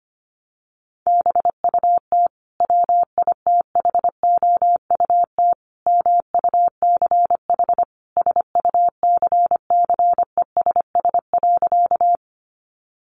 Morse code, heard as a keyed tone, about 25 wpm, 700 Hz. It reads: BUT WITHOUT MUCH SUCCESS.
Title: Sentences from Top 900 Words 25wpm